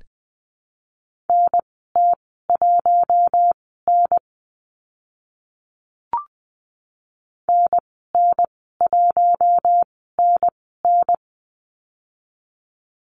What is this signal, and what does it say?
Morse code, heard as a keyed tone, about 20 wpm, spaced as for 14 wpm, 700 Hz. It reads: NT1N NN1NN
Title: 042 Lesson 8g - Number 1 - Call Signs 20wpm